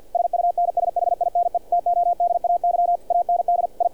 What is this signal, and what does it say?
Morse code, heard as a keyed tone, about 48 wpm, 700 Hz, with radio noise. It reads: SUNSHINE TODAY AND I
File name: archived / CRSnet